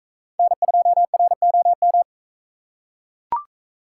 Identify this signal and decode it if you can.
Morse code, heard as a keyed tone, about 43 wpm, 700 Hz. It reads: N2ROM